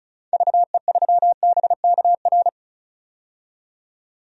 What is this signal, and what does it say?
Morse code, heard as a keyed tone, about 36 wpm, 690 Hz. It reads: VE3BKR
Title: DXCC-100 Call Signs - 6 Characters - 35wpm